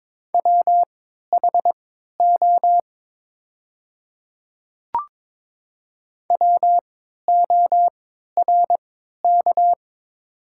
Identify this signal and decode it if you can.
Morse code, heard as a keyed tone, about 22 wpm, spaced as for 12 wpm, 690 Hz. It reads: WHO WORK
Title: 3x Character Spacing - Top 500 Words - 22wpm